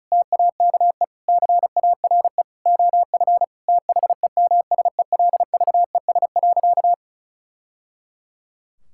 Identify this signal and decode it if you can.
Morse code, heard as a keyed tone, about 36 wpm, 700 Hz. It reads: TAKE CARE OF THEMSELVES.
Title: Sentences from Top 1000 Words - Review Part 5 - 35wpm